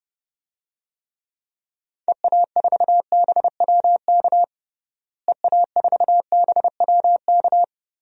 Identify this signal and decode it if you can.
Morse code, heard as a keyed tone, about 30 wpm, 700 Hz. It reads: EA4BWK EA4BWK
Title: DXCC-100 Call Signs - 6 Characters - Repeated 2x - 30wpm